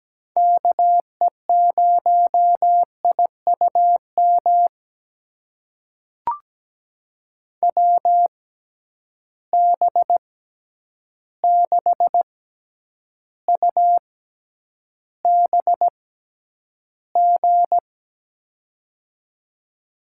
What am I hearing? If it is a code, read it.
Morse code, heard as a keyed tone, about 17 wpm, spaced as for 9 wpm, 700 Hz. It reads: KE0IUM WB6UBG
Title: USA Call Signs - 6 Characters - 6x Character Spacing - 17wpm